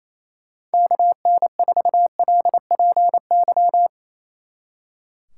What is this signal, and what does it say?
Morse code, heard as a keyed tone, about 30 wpm, 700 Hz. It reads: KN4LPY
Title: Speed Racing - USA Call Signs 28wpm